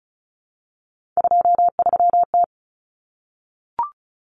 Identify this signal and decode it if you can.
Morse code, heard as a keyed tone, about 34 wpm, 700 Hz. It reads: 23T
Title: Common Cut Numbers - 3-Digit 35wpm